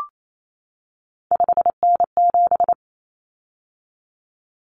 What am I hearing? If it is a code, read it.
Morse code, heard as a keyed tone, about 28 wpm, 700 Hz. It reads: 5N7
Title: Common Cut Numbers - 3-Digit 28wpm